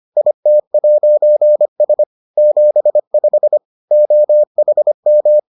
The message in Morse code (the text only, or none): IT'S 75 OHM